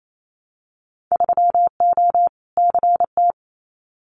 Morse code, heard as a keyed tone, about 28 wpm, 690 Hz. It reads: 3O CT
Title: Winter Field Day Exchanges Encore 28wpm